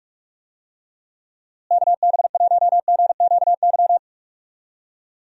Morse code, heard as a keyed tone, about 46 wpm, 700 Hz. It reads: KB1GQY
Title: USA Call Signs - 6 Characters - Repeated 3x - 45wpm